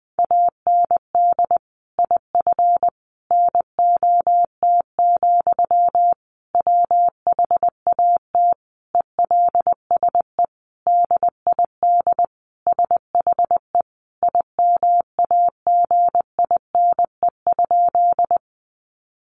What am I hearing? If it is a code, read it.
Morse code, heard as a keyed tone, about 20 wpm, 700 Hz. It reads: AND IF NOT, WHAT ELSE DID SHE IMAGINE?
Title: Sentences from Top 1400 Words 20wpm